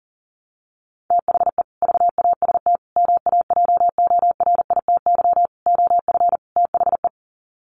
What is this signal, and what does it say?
Morse code, heard as a keyed tone, about 40 wpm, 700 Hz. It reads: THE VAST MAJORITY OF THE
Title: Sets of 5 Words Encore 40wpm